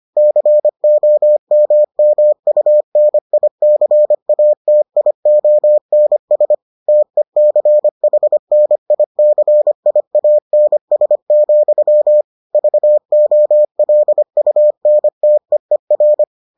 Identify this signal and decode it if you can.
Morse code, heard as a keyed tone, about 25 wpm, 600 Hz. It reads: COMMUNICATIONS TECHNICIANS, VOLUNTEER